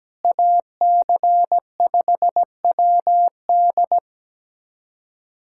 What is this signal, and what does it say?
Morse code, heard as a keyed tone, about 17 wpm, 700 Hz. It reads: AC5WD
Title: USA Callsign DE Callsign - 5 Characters - 17wpm